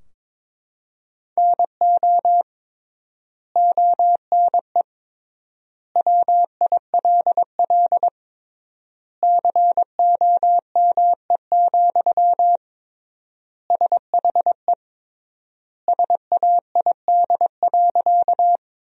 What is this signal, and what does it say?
Morse code, heard as a keyed tone, about 21 wpm, 700 Hz. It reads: NO ONE WILL COME, SHE SAID.
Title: 3x Word-Spacing - Sentences from Top 100 Words - 22wpm